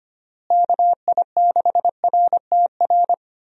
Morse code, heard as a keyed tone, about 25 wpm, 700 Hz. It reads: KI6RTR
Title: USA Call Signs - 6 Characters - Repeated 2x - 25wpm